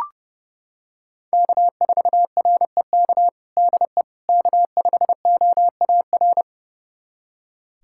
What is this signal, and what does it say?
Morse code, heard as a keyed tone, about 30 wpm, 700 Hz. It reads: K4REK DE K5OAR
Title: USA Callsign DE Callsign - 5 Characters - 30wpm